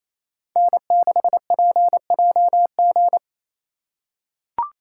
Morse code, heard as a keyed tone, about 27 wpm, 700 Hz. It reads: N6PJG
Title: USA Call Signs - 2x Character Spacing - 28wpm